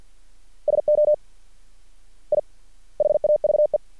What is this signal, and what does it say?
Morse code, heard as a keyed tone, about 50 wpm, 600 Hz. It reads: SO I HAVE
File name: archived / CRSnet